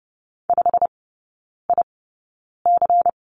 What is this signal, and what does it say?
Morse code, heard as a keyed tone, about 30 wpm, spaced as for 9 wpm, 700 Hz. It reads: 5IC
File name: Lesson 18e - Letter m - Three Characters 30wpm